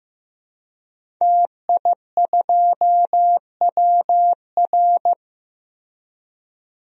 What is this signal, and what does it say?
Morse code, heard as a keyed tone, about 15 wpm, 700 Hz. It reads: TI2WR
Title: DXCC-100 Call Signs - 5 Characters - Repeated 3x - 15wpm